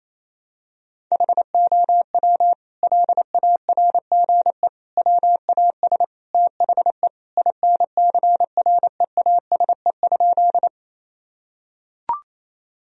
Morse code, heard as a keyed tone, about 28 wpm, 700 Hz. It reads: HOW LARGE WAS THE INCREASE?
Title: Sentences from Top 1000 Words - Review Part 4 - 28wpm